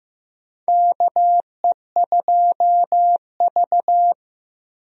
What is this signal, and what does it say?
Morse code, heard as a keyed tone, about 15 wpm, 700 Hz. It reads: KE2V
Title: USA Call Signs - 4 Characters - Repeated 2x - 15wpm